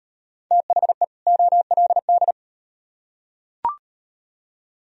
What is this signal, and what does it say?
Morse code, heard as a keyed tone, about 36 wpm, 700 Hz. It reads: THE OLD